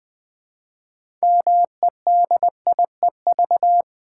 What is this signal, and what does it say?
Morse code, heard as a keyed tone, about 20 wpm, 700 Hz. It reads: MEDIEV
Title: Build A Word - 2 Letters At A Time - 8-Letter Words - 20wpm